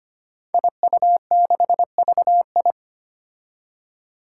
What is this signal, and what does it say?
Morse code, heard as a keyed tone, about 25 wpm, 700 Hz. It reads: IU6VI
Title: Five Challenging Letter-Number Combinations 25wpm